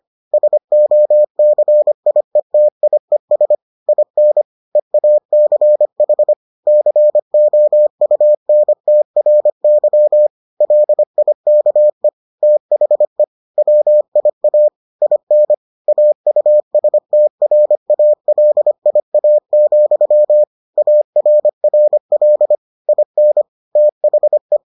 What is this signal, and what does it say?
Morse code, heard as a keyed tone, about 25 wpm, 600 Hz. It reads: SOCIETIES IN EACH COUNTRY LIKE THE WIA IN AUSTRALIA, ARRL IN THE